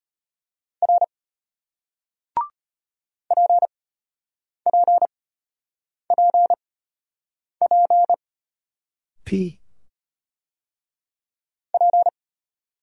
Morse code, heard as a keyed tone, about 34 wpm, spaced as for 8 wpm, 700 Hz. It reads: R PPPP P